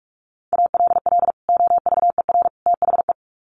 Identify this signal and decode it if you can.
Morse code, heard as a keyed tone, about 44 wpm, 700 Hz. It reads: ALL OVER THE
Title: Sets of 3 Words 45wpm